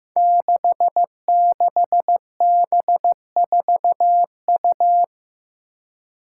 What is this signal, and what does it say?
Morse code, heard as a keyed tone, about 15 wpm, 700 Hz. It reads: 66B4U
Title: Five Challenging Letter-Number Combinations 15wpm